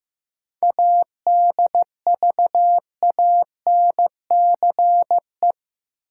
Morse code, heard as a keyed tone, about 15 wpm, 700 Hz. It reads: ADVANCE
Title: Sentences from Top 1400 Words - New Word Drill - 15wpm